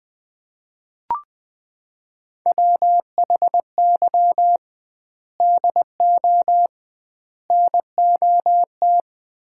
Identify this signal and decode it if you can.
Morse code, heard as a keyed tone, about 20 wpm, 700 Hz. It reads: WHY DO NOT
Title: 2x Word-Spacing - Sets of 5 Words - 20wpm